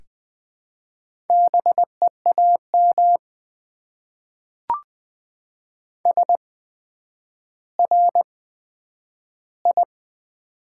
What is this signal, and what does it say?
Morse code, heard as a keyed tone, about 20 wpm, spaced as for 10 wpm, 700 Hz. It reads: BEAM SRI
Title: Top 90 QSO Elements - 8x Character Spacing - 20wpm